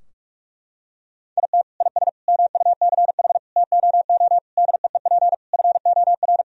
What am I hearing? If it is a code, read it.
Morse code, heard as a keyed tone, about 46 wpm, 700 Hz. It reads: IT IS MUCH TOO DEEP FOR
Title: Sentences from Top 1000 Words - Review Part 3 - 45wpm